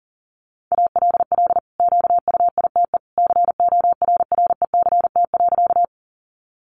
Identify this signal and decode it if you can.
Morse code, heard as a keyed tone, about 40 wpm, 700 Hz. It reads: ALL QUITE CORRECT.
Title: Sentences from Top 700 Words 40wpm